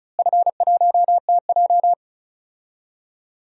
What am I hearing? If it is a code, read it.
Morse code, heard as a keyed tone, about 34 wpm, 700 Hz. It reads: F1TJ
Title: DXCC-100 Call Signs - 4 Characters - Repeated 2x - 35wpm